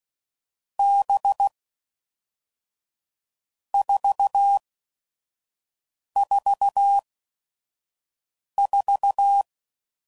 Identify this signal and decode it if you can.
Morse code, heard as a keyed tone, about 16 wpm, spaced as for 5 wpm, 800 Hz. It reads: B 444
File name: Groups / Basic Letter Groups / VB46